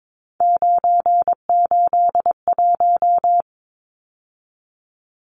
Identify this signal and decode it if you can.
Morse code, heard as a keyed tone, about 22 wpm, 700 Hz. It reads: 981
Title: Numbers - 3-Digit 22wpm